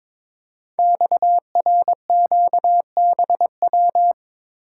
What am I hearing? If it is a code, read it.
Morse code, heard as a keyed tone, about 23 wpm, 700 Hz. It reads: XRQBW
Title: Code Groups - 5 Letters - 22wpm